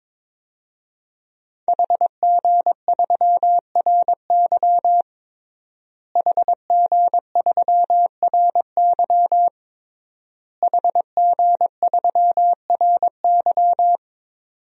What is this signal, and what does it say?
Morse code, heard as a keyed tone, about 22 wpm, 700 Hz. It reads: HG3RY HG3RY HG3RY
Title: DXCC-100 Call Signs - 5 Characters - Repeated 3x - 22wpm